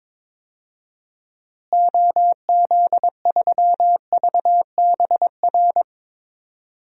Morse code, heard as a keyed tone, about 22 wpm, 700 Hz. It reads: OZ3VBR
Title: DXCC-100 Call Signs - 6 Characters - Repeated 2x - 22wpm